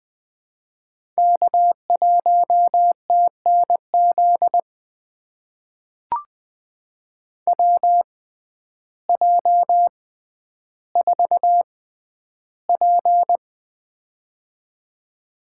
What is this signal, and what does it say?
Morse code, heard as a keyed tone, about 20 wpm, spaced as for 11 wpm, 700 Hz. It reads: K1TNZ WJ4P
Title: USA Call Signs - 6x Character Spacing - 20wpm